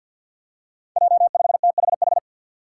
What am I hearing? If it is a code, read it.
Morse code, heard as a keyed tone, about 52 wpm, 700 Hz. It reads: J5THH
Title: Five Challenging Letter-Number Combinations 50wpm